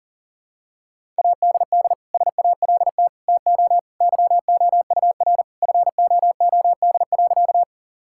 Morse code, heard as a keyed tone, about 37 wpm, 700 Hz. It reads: ADD SALT TO YOUR FOOD.
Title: Sentences from Top 1000 Words - Review Part 5 - 40wpm